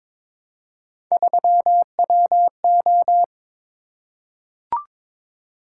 Morse code, heard as a keyed tone, about 22 wpm, 700 Hz. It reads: 3WO
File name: practice / 22wpm / Three Letter-Number Combinations - 22wpm